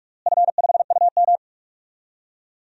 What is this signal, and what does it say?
Morse code, heard as a keyed tone, about 44 wpm, 700 Hz. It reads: F5UM
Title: DXCC-100 Call Signs - 4 Characters - Repeated 2x - 45wpm